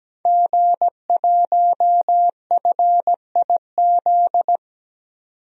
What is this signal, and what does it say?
Morse code, heard as a keyed tone, about 17 wpm, 700 Hz. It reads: G1FIZ
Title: DXCC-100 Call Signs - 5 Characters - 17wpm